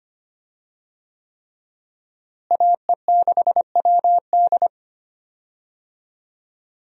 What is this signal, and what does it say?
Morse code, heard as a keyed tone, about 25 wpm, 700 Hz. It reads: AE6WD
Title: USA Call Signs - 5 Characters - Repeated 3x - 25wpm